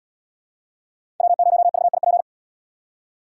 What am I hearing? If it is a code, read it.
Morse code, heard as a keyed tone, about 78 wpm, 700 Hz. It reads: W0VIP